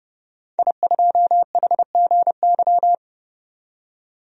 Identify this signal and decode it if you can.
Morse code, heard as a keyed tone, about 29 wpm, 700 Hz. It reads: I2HGY
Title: DXCC-100 Call Signs - 5 Characters - 30wpm